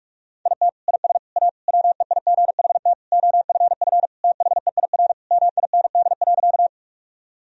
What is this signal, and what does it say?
Morse code, heard as a keyed tone, about 41 wpm, 700 Hz. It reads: IT IS A WEIGHT OFF THEIR MIND.
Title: Sentences from Top 1000 Words - Review Part 3 - 45wpm